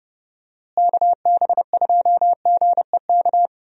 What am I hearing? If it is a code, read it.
Morse code, heard as a keyed tone, about 31 wpm, 700 Hz. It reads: KB2GEK
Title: USA Call Signs - 6 Characters - Repeated 2x - 30wpm